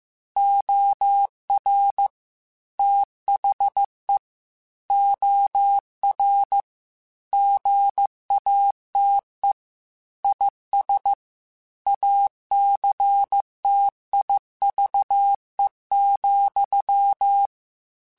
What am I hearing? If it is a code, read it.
Morse code, heard as a keyed tone, about 15 wpm, 800 Hz. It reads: OR THE OR GATE IS ACTIVE,
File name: Technical Subjects / Technical Subjects-mp3s / AND Gate @13